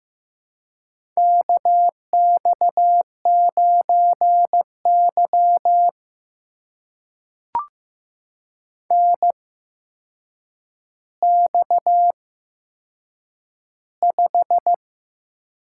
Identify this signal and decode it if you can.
Morse code, heard as a keyed tone, about 15 wpm, spaced as for 7 wpm, 700 Hz. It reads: KX9Y NX5
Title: USA Call Signs - 4 Characters - 8x Character Spacing - 15wpm